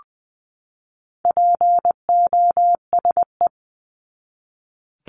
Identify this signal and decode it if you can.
Morse code, heard as a keyed tone, about 20 wpm, 700 Hz. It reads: POSE
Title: 260 Lesson 42e - Words 20-20wpm